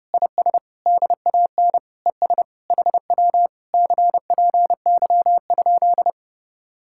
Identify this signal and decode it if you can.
Morse code, heard as a keyed tone, about 30 wpm, 700 Hz. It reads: IS DAN ES HW CPY?